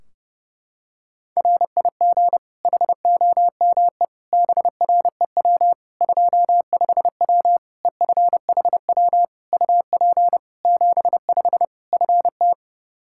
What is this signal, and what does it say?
Morse code, heard as a keyed tone, about 30 wpm, 700 Hz. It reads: RIG HOME BREW 25W EFHW UP 75 FT
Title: K1USN SST QSO Practice 30wpm